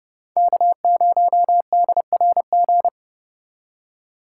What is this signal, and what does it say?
Morse code, heard as a keyed tone, about 30 wpm, 700 Hz. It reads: K0DRG